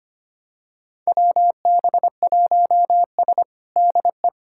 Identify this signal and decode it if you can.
Morse code, heard as a keyed tone, about 25 wpm, 700 Hz. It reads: WB1S DE
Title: USA Callsign DE Callsign - 4 Characters - 25wpm